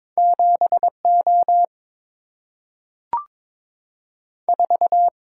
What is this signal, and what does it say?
Morse code, heard as a keyed tone, about 22 wpm, 700 Hz. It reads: 7 O 4
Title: Single-Letter Single-Number 22wpm